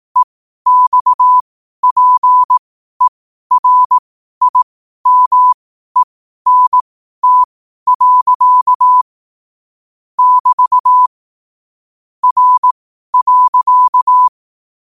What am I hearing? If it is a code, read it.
Morse code, heard as a keyed tone, about 18 wpm, spaced as for 12 wpm, 1000 Hz. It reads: EXPERIMENT. <BT> R.
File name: qotdfiles / 12wpm